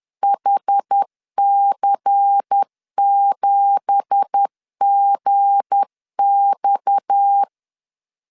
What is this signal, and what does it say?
Morse code, heard as a keyed tone, about 10 wpm, 800 Hz. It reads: HC7GX
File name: LEZIONI / 9 2867 10wpm 2733